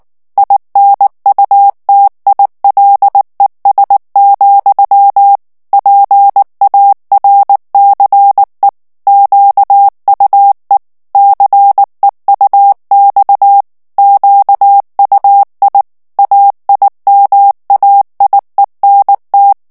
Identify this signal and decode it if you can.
Morse code, heard as a keyed tone, about 19 wpm, 800 Hz. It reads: INUTILES, PARCE QUE CEUX QUI AIMAIENT
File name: CW / FICHIERS-MP3 / FRANCAIS TEXTE CLAIR 2 VITESSE 18 MOTS MINUTES